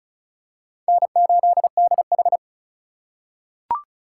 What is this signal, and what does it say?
Morse code, heard as a keyed tone, about 36 wpm, 700 Hz. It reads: N8DH
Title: USA Call Signs - 4 Characters - 3x Character Spacing - 35wpm